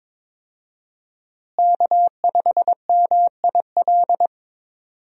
Morse code, heard as a keyed tone, about 21 wpm, 700 Hz. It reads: K5MIL
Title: USA Call Signs - 5 Characters - Repeated 2x - 22wpm